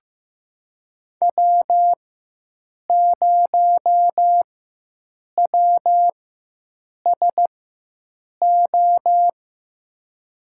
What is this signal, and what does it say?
Morse code, heard as a keyed tone, about 15 wpm, spaced as for 7 wpm, 700 Hz. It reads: W0WSO
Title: USA Call Signs - 4x Character Spacing - 15wpm